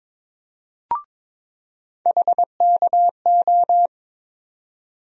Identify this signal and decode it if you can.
Morse code, heard as a keyed tone, about 22 wpm, 700 Hz. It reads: HKO
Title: USA Call Sign Postfixes - 3 Letters 22wpm